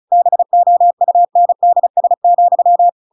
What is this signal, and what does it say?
Morse code, heard as a keyed tone, about 35 wpm, 700 Hz. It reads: BOUNDS,
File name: Pinocchio / Pinocchio ch6 35 35 ik1hge